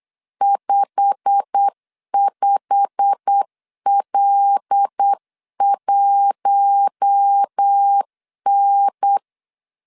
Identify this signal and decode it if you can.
Morse code, heard as a keyed tone, about 8 wpm, 800 Hz. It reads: 55L1N
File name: CW / DET / 1 9 1500C S A 8wpm 5637